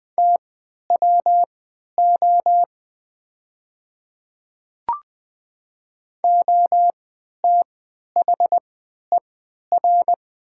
Morse code, heard as a keyed tone, about 20 wpm, spaced as for 11 wpm, 700 Hz. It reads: TWO OTHER
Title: 3x Character Spacing - Top 100 Words - 20wpm